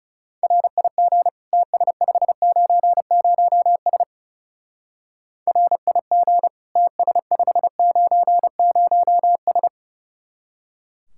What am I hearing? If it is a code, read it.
Morse code, heard as a keyed tone, about 37 wpm, 700 Hz. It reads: RIG TS590S RIG TS590S
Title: Speed-Racing Common QSO Phrases - 30wpm